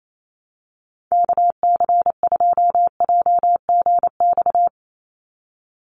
Morse code, heard as a keyed tone, about 28 wpm, 700 Hz. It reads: KC2JGX
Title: USA Call Signs 28wpm